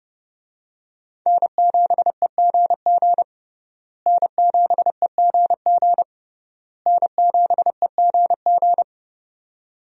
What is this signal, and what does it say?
Morse code, heard as a keyed tone, about 32 wpm, 700 Hz. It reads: N7EGG N7EGG N7EGG
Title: USA Call Signs - 5 Characters - Repeated 3x - 30wpm